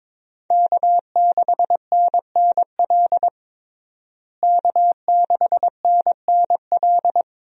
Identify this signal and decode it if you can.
Morse code, heard as a keyed tone, about 22 wpm, 700 Hz. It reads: K6NNL K6NNL
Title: USA Call Signs - 5 Characters - Repeated 3x - 22wpm